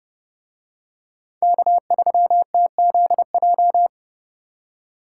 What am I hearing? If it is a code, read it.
Morse code, heard as a keyed tone, about 30 wpm, 700 Hz. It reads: K3TZJ